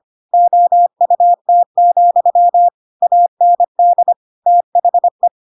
Morse code, heard as a keyed tone, about 25 wpm, 700 Hz. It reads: OUT, AND THE
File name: Pinocchio / Pinocchio ch7 25 25 ik1hge